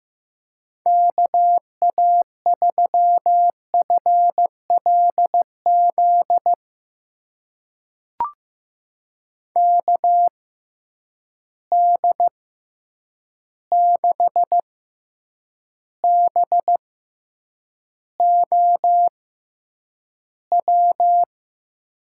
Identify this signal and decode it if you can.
Morse code, heard as a keyed tone, about 15 wpm, spaced as for 8 wpm, 700 Hz. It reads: KA3FLZ KD6BOW
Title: USA Call Signs - 6 Characters - 6x Character Spacing - 15wpm